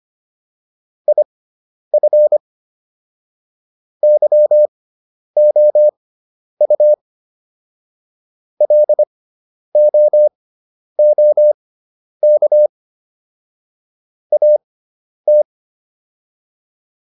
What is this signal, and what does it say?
Morse code, heard as a keyed tone, about 25 wpm, spaced as for 10 wpm, 600 Hz. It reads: IF YOU LOOK AT
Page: intermediate-practice-files